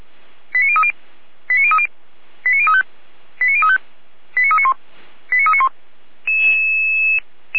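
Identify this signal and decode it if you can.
Five-tone selective call (ZVEI), heard as quick runs of five tones, tones of about 70 ms. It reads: 89030 89030 89035 89035 89391 89391
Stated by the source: ZVEI five-tone call 89509 received by SDR.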